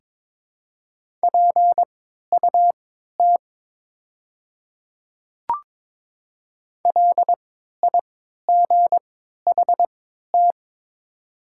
Morse code, heard as a keyed tone, about 22 wpm, spaced as for 12 wpm, 700 Hz. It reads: PUT LIGHT
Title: 3x Character Spacing - Top 300 Words - 22wpm